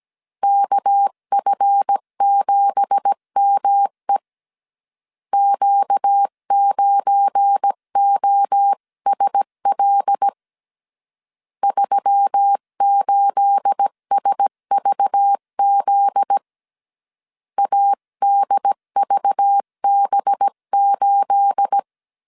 Morse code, heard as a keyed tone, about 17 wpm, 800 Hz. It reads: KF7ME Q9OSL 38SVZ ADVB8
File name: CW / DET / 1 9 1500C S B 16wpm 2856